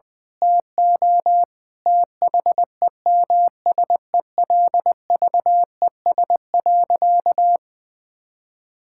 Morse code, heard as a keyed tone, about 20 wpm, 700 Hz. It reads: TO THEMSELVES.
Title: Sentences from Top 1000 Words - Review Part 6 - 20wpm